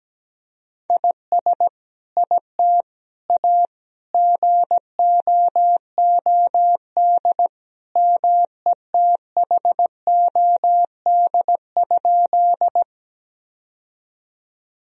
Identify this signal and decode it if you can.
Morse code, heard as a keyed tone, about 17 wpm, 700 Hz. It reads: IS IT A GOOD METHOD?
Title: Sentences from Top 1000 Words - Review Part 3 - 17wpm